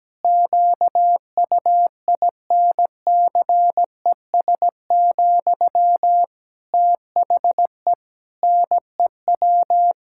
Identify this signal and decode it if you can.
Morse code, heard as a keyed tone, about 17 wpm, 700 Hz. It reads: QUINCES, THE NEW
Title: Pangram Challenge v2 - 17wpm